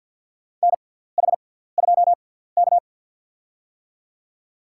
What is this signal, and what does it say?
Morse code, heard as a keyed tone, about 52 wpm, spaced as for 17 wpm, 700 Hz. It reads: NH2K
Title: USA Call Signs - 4 Characters - 6x Character Spacing - 50wpm